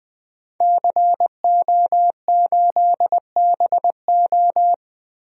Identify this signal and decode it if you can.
Morse code, heard as a keyed tone, about 20 wpm, 700 Hz. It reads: CO8BO
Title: DXCC-100 Call Signs - 5 Characters - 20wpm